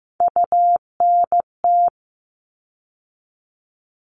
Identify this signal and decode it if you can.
Morse code, heard as a keyed tone, about 15 wpm, 700 Hz. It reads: UNT
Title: All Cut Numbers - 3-Digit 15wpm